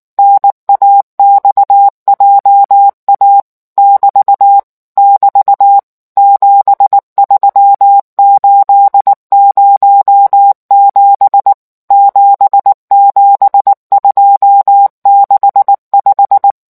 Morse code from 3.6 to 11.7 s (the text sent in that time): <BT> <BT> 73807